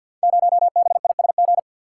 Morse code, heard as a keyed tone, about 49 wpm, 690 Hz. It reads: 0BISG
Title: Code Groups - 5 Letter:Numbers - 50wpm